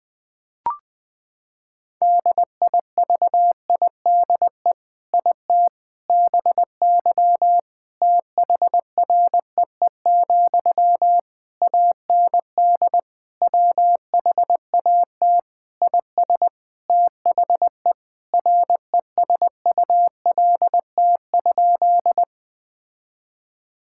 Morse code, heard as a keyed tone, about 20 wpm, 700 Hz. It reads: DIVIDE IT BY THREE, AND WHAT IS THE RESULT?
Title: Sentences from Top 1600 Words 20wpm